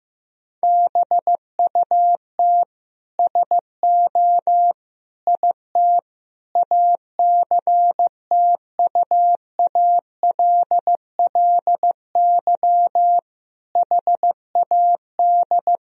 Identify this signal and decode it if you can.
Morse code, heard as a keyed tone, about 15 wpm, 700 Hz. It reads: BUT SO IT ACTUALLY HAD
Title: Sentences from Top 1000 Words - Review Part 4 - 15wpm